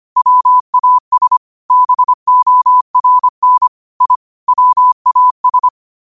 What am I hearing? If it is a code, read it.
Morse code, heard as a keyed tone, about 25 wpm, 1000 Hz. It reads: WAS BORN I WAS